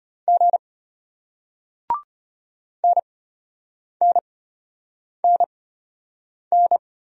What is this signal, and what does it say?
Morse code, heard as a keyed tone, about 34 wpm, spaced as for 8 wpm, 700 Hz. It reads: G NNNN